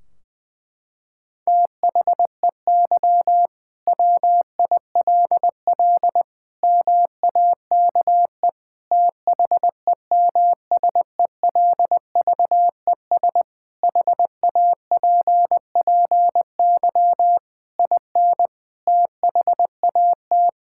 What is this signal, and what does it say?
Morse code, heard as a keyed tone, about 20 wpm, 700 Hz. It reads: THEY WILL MAKE THEMSELVES HAPPY IN THAT
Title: Sentences from Top 1600 Words 20wpm